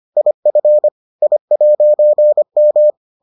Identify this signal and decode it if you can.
Morse code, heard as a keyed tone, about 25 wpm, 600 Hz. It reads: IF I'M